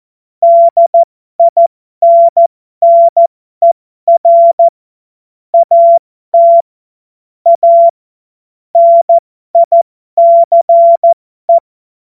Morse code, heard as a keyed tone, about 14 wpm, spaced as for 12 wpm, 680 Hz. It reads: DINNER AT A NICE
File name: Morsekurs / Witze